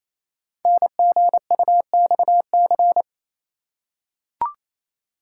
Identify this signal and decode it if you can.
Morse code, heard as a keyed tone, about 29 wpm, 700 Hz. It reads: NGUXC
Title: Code Groups - 5 Letter:Numbers - 28wpm